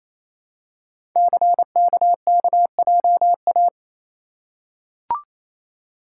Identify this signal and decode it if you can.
Morse code, heard as a keyed tone, about 28 wpm, 700 Hz. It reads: CKKJA
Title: Code Groups - 5 Letters - 28wpm